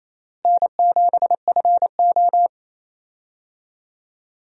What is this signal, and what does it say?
Morse code, heard as a keyed tone, about 28 wpm, 700 Hz. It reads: N7FO
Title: USA Call Signs - 4 Characters - Repeated 3x - 28wpm